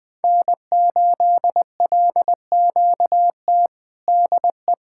Morse code, heard as a keyed tone, about 20 wpm, 700 Hz. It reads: N8LQT DE
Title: K1USN SST QSO Practice 20wpm